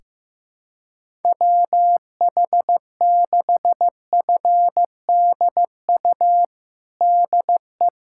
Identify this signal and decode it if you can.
Morse code, heard as a keyed tone, about 15 wpm, 700 Hz. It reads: WH6FDU DE
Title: USA Callsign DE Callsign - 6 Characters - 15wpm